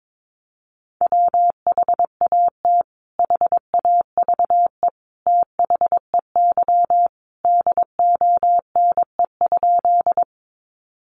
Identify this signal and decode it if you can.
Morse code, heard as a keyed tone, about 22 wpm, 700 Hz. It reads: WHAT HAVE THEY DONE?
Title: Sentences from Top 500 Words 22wpm